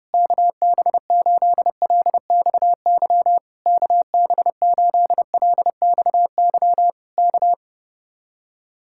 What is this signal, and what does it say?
Morse code, heard as a keyed tone, about 29 wpm, 700 Hz. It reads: KB8LXY KB8LXY K
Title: Speed-Racing Common QSO Phrases Encore - 30wpm